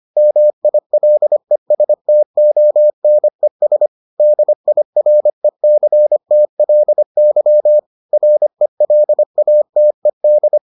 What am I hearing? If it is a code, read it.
Morse code, heard as a keyed tone, about 25 wpm, 600 Hz. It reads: MILESTONES DIRECTLY RELATED